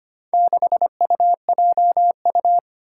Morse code, heard as a keyed tone, about 25 wpm, 700 Hz. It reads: 6UJU
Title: Four Challenging Letter-Number Combinations 25wpm